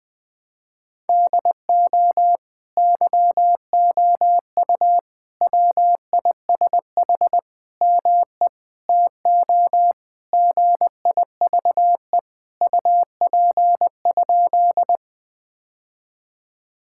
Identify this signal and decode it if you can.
Morse code, heard as a keyed tone, about 20 wpm, 700 Hz. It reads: DO YOU WISH ME TO GIVE UP?
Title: Sentences from Top 1600 Words 20wpm